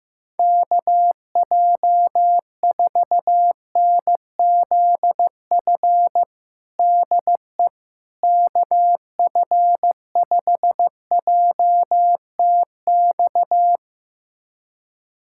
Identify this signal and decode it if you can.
Morse code, heard as a keyed tone, about 15 wpm, 700 Hz. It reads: KJ4NZF DE KF5JTX
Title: USA Callsign DE Callsign - 6 Characters - 15wpm